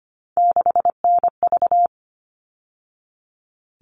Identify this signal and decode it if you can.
Morse code, heard as a keyed tone, about 25 wpm, 700 Hz. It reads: 6NV
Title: All Cut Numbers - 3-Digit 25wpm